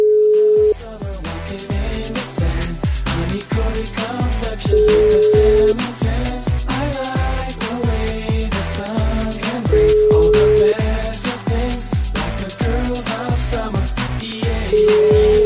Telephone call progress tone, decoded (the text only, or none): ringback tone